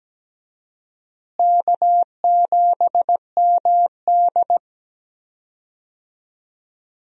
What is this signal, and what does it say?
Morse code, heard as a keyed tone, about 17 wpm, 700 Hz. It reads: K7MD
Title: USA Call Signs - 4 Characters - Repeated 2x - 17wpm